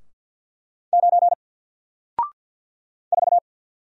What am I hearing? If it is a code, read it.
Morse code, heard as a keyed tone, about 48 wpm, 700 Hz. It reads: 9 4
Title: Single Letter-Number Alt 50wpm